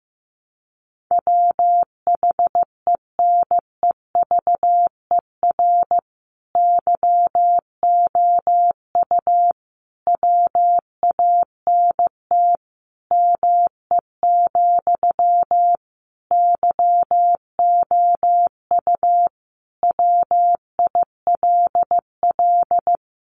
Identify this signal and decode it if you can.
Morse code, heard as a keyed tone, about 15 wpm, 700 Hz. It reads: WHENEVER YOU WANT ME, YOU WILL
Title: Sentences from Top 1200 Words 15wpm